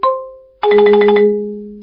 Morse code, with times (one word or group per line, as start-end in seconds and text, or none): none